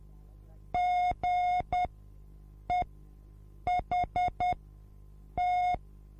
Morse code, heard as a keyed tone, about 10 wpm, spaced as for 7 wpm, 720 Hz. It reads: GEHT
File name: repos / Morsekurs / DJ4UF Morsen24